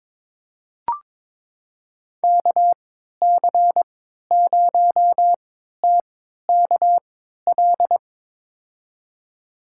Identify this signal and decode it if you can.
Morse code, heard as a keyed tone, about 22 wpm, spaced as for 12 wpm, 700 Hz. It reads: KC0TKL
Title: USA Call Signs - 6 Characters - 3x Character Spacing - 22wpm